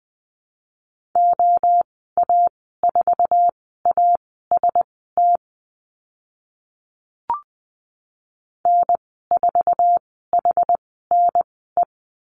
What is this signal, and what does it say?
Morse code, heard as a keyed tone, about 20 wpm, spaced as for 14 wpm, 700 Hz. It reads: OA4AST N4HNE
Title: 069 Lesson 12g - Letter d - Call Signs 20wpm